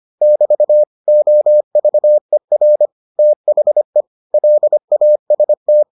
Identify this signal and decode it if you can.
Morse code, heard as a keyed tone, about 25 wpm, 600 Hz. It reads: <BT> OVER THE LAST